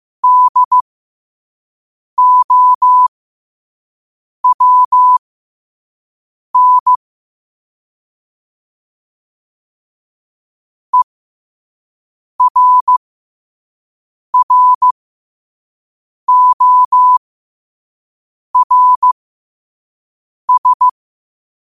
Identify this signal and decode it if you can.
Morse code, heard as a keyed tone, about 15 wpm, spaced as for 5 wpm, 1000 Hz. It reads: DOWN ERRORS